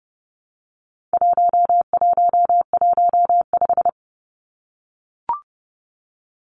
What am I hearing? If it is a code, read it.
Morse code, heard as a keyed tone, about 30 wpm, 700 Hz. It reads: 1115
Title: Numbers - 4-Digit 30wpm